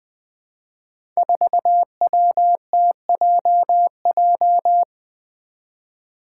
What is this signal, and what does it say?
Morse code, heard as a keyed tone, about 20 wpm, 700 Hz. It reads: 4WTJJ
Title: Five Challenging Letter-Number Combinations 20wpm